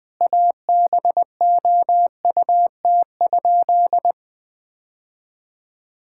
Morse code, heard as a keyed tone, about 20 wpm, 700 Hz. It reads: ABOUT?
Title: Sentences with 2-Word Phrases - Spoken Beginning - 20wpm